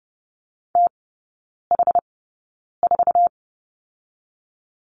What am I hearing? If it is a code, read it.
Morse code, heard as a keyed tone, about 30 wpm, spaced as for 9 wpm, 700 Hz. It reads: TH4
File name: Lesson 18e - Letter m - Three Characters 30wpm